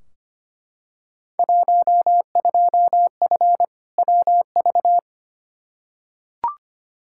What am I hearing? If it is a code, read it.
Morse code, heard as a keyed tone, about 25 wpm, 700 Hz. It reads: 12F WV
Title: Field Day Exchanges Encore - 25wpm